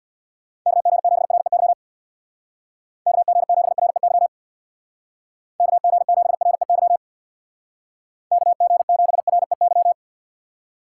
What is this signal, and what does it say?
Morse code, heard as a keyed tone, about 62 wpm, 700 Hz. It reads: KG7REY KG7REY KG7REY KG7REY
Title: Speed Racing - USA 2x3 Call Signs 50wpm